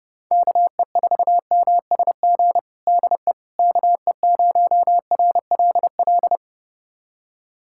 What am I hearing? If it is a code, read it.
Morse code, heard as a keyed tone, about 32 wpm, 700 Hz. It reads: KE4MSG DE KE0RLL
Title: USA Callsign DE Callsign - 6 Characters - 30wpm